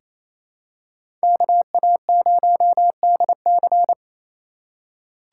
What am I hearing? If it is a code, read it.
Morse code, heard as a keyed tone, about 28 wpm, 700 Hz. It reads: KA0DC
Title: USA Call Signs - 5 Characters - 6x Character Spacing - 28wpm